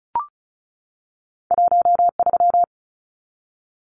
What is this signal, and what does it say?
Morse code, heard as a keyed tone, about 36 wpm, 700 Hz. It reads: 13
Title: Numbers - 2-Digit 35wpm